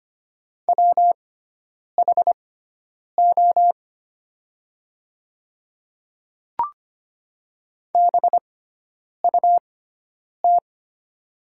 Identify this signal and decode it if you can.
Morse code, heard as a keyed tone, about 25 wpm, spaced as for 9 wpm, 700 Hz. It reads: WHO BUT
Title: 6x Character Spacing - Top 100 Words - 25wpm